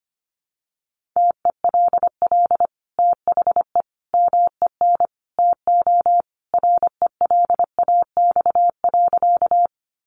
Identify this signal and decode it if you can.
Morse code, heard as a keyed tone, about 25 wpm, 700 Hz. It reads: TELL THE MEN TO RELAX.
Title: Sentences from Top 1400 Words 25wpm